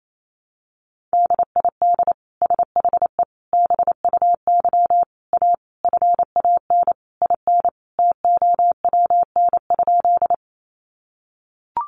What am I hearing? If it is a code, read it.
Morse code, heard as a keyed tone, about 28 wpm, 700 Hz. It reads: DID SHE BUY A FAN IN TOWN?
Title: Sentences from Top 1300 Words 28wpm